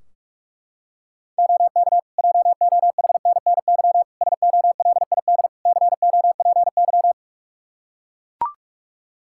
Morse code, heard as a keyed tone, about 44 wpm, 700 Hz. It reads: OK JOHNNY SOLID COPY
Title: Practice QSO - Exchange 3 - More Info - 45wpm